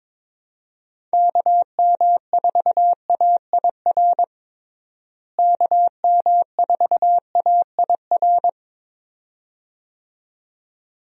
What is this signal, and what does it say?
Morse code, heard as a keyed tone, about 22 wpm, 700 Hz. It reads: KM4AIR KM4AIR
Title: USA Call Signs - 6 Characters - Repeated 2x - 22wpm